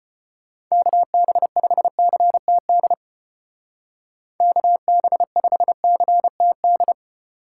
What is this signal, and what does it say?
Morse code, heard as a keyed tone, about 32 wpm, 700 Hz. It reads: KB5CTD KB5CTD